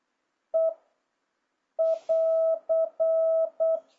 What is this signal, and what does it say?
Morse code, heard as a keyed tone, about 8 wpm, 650 Hz. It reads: E <AR>
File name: jh8cbh / 2026cw